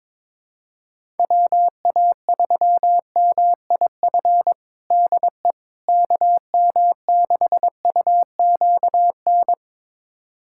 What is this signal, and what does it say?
Morse code, heard as a keyed tone, about 22 wpm, 700 Hz. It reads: WA3MIF DE KM6UQN
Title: USA Callsign DE Callsign - 6 Characters - 22wpm